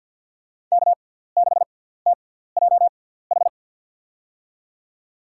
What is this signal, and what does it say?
Morse code, heard as a keyed tone, about 47 wpm, spaced as for 17 wpm, 690 Hz. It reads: K6TJH
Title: USA Call Signs - 5 Characters - 6x Character Spacing - 50wpm